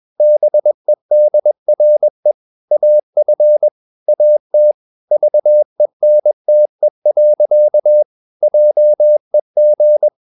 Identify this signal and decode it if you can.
Morse code, heard as a keyed tone, about 21 wpm, 600 Hz. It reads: BEDRE AF AT VENTE. JEG
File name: Ferie QRS 21wpm